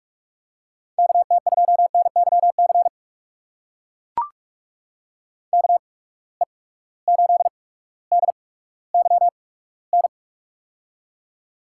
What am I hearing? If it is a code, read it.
Morse code, heard as a keyed tone, about 43 wpm, spaced as for 19 wpm, 700 Hz. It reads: KT2NYC KE8DYN
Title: USA Call Signs - 6 Characters - 8x Character Spacing - 45wpm